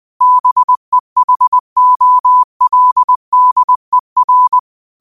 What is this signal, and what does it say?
Morse code, heard as a keyed tone, about 20 wpm, 1000 Hz. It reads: BEHOLDER